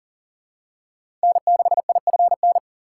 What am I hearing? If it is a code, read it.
Morse code, heard as a keyed tone, about 40 wpm, 700 Hz. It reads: N6IFN
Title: USA Call Signs - 5 Characters - 4x Character Spacing - 40wpm